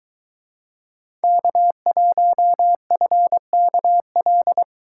Morse code, heard as a keyed tone, about 22 wpm, 700 Hz. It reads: K1FKL